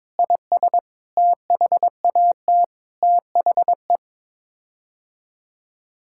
Morse code, heard as a keyed tone, about 22 wpm, 700 Hz. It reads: IS THAT THE
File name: practice / 22wpm / Sets of 7 Words - 22wpm